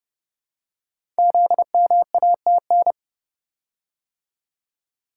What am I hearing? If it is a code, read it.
Morse code, heard as a keyed tone, about 30 wpm, 700 Hz. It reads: ZMATN
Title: Code Groups - 5 Letters - 30wpm